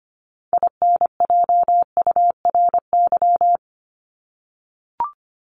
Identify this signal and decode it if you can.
Morse code, heard as a keyed tone, about 25 wpm, 700 Hz. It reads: INJURY
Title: Top 1751-2000 Words 25wpm